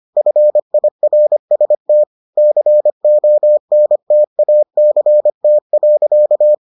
Morse code, heard as a keyed tone, about 25 wpm, 600 Hz. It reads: FIRST CONTACT.